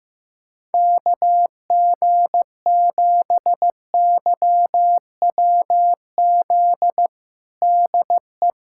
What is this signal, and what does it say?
Morse code, heard as a keyed tone, about 15 wpm, 700 Hz. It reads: KG7YWZ DE
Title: USA Callsign DE Callsign - 6 Characters - 15wpm